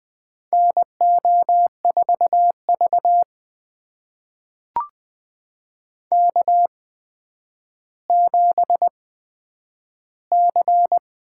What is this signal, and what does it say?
Morse code, heard as a keyed tone, about 20 wpm, spaced as for 10 wpm, 700 Hz. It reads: NO4V K7C
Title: USA Call Signs - 4 Characters - 8x Character Spacing - 20wpm